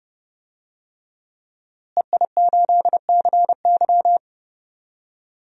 Morse code, heard as a keyed tone, about 32 wpm, 700 Hz. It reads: EI8CY
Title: DXCC-100 Call Signs - 5 Characters - Repeated 3x - 30wpm